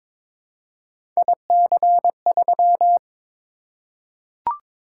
Morse code, heard as a keyed tone, about 22 wpm, 700 Hz. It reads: IC3
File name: practice / 22wpm / Three Letter-Number Combinations - 22wpm